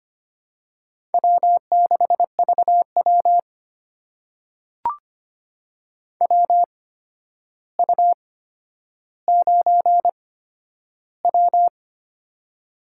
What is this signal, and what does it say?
Morse code, heard as a keyed tone, about 25 wpm, spaced as for 11 wpm, 700 Hz. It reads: W6VW WU9W
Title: USA Call Signs - 4 Characters - 8x Character Spacing - 25wpm